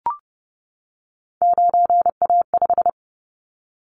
Morse code, heard as a keyed tone, about 30 wpm, 700 Hz. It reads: 9A5
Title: Callsign Prefixes for Top 25 DXCC - 30wpm